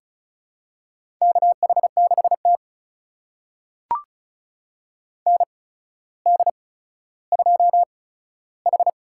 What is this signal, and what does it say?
Morse code, heard as a keyed tone, about 34 wpm, spaced as for 15 wpm, 700 Hz. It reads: KH6T ND2H
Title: USA Call Signs - 4 Characters - 8x Character Spacing - 35wpm